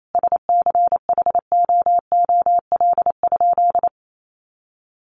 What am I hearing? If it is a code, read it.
Morse code, heard as a keyed tone, about 28 wpm, 700 Hz. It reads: SCHOOL?
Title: Sentences from Top 1400 Words 28wpm